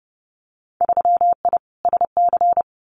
Morse code, heard as a keyed tone, about 30 wpm, 700 Hz. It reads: 3I SC
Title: Winter Field Day Exchanges 30wpm